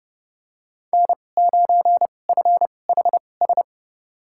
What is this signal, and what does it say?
Morse code, heard as a keyed tone, about 30 wpm, spaced as for 22 wpm, 700 Hz. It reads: N9FHS
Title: USA Call Signs - 2x Character Spacing - 30wpm